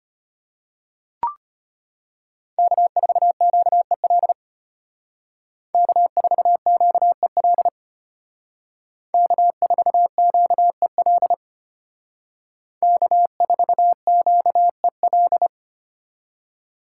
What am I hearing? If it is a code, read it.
Morse code, heard as a keyed tone, about 33 wpm, 700 Hz. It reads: K4QEL K4QEL K4QEL K4QEL